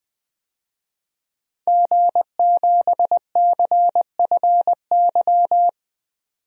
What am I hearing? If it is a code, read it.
Morse code, heard as a keyed tone, about 20 wpm, 700 Hz. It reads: G7CFY
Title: DXCC-100 Call Signs - 5 Characters - 20wpm